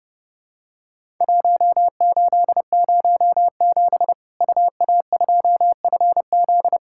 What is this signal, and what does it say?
Morse code, heard as a keyed tone, about 32 wpm, 700 Hz. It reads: 1807 UA2FZ
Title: Worked All Europe DX Contest - QTC Exchange - 30wpm